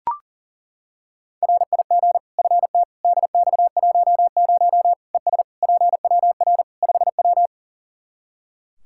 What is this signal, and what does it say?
Morse code, heard as a keyed tone, about 39 wpm, 700 Hz. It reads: RIG FT DX10 ES PWR 5W
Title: Practice QSO - Exchange 2 - Info - 40wpm